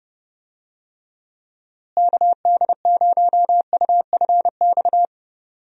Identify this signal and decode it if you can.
Morse code, heard as a keyed tone, about 31 wpm, 700 Hz. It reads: KD0UFX
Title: USA Call Signs - 6 Characters - Repeated 2x - 30wpm